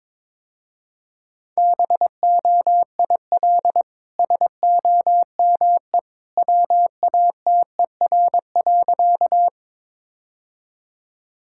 Morse code, heard as a keyed tone, about 22 wpm, 700 Hz. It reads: BOIL SOME WATER.
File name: practice / 22wpm / Easy Sentences with Hints - 22wpm